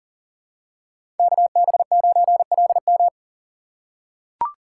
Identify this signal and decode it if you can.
Morse code, heard as a keyed tone, about 40 wpm, 700 Hz. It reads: KB9LM
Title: USA Call Signs - 5 Characters - 6x Character Spacing - 40wpm